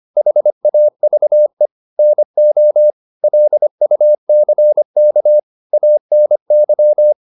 Morse code, heard as a keyed tone, about 25 wpm, 600 Hz. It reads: HAVE NO LUCK ANY
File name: the old man and the sea part 1 25wpm